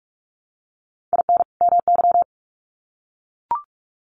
Morse code, heard as a keyed tone, about 44 wpm, 700 Hz. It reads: IN MY
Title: Sets of 2 Words 45wpm